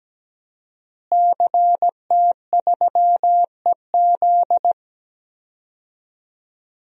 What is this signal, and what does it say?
Morse code, heard as a keyed tone, about 17 wpm, 700 Hz. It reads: CT3EZ
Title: DXCC-100 Call Signs - 5 Characters - Repeated 3x - 17wpm